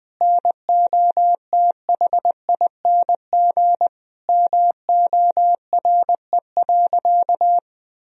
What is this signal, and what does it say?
Morse code, heard as a keyed tone, about 20 wpm, 700 Hz. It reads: NOTHING MORE.
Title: Sentences from Top 1000 Words - Review Part 1 - 20wpm